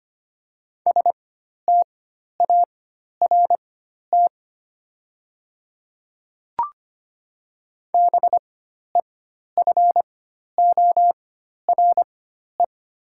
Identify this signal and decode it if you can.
Morse code, heard as a keyed tone, about 25 wpm, spaced as for 12 wpm, 700 Hz. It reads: START BEFORE
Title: 4x Character Spacing - Top 300 Words - 25wpm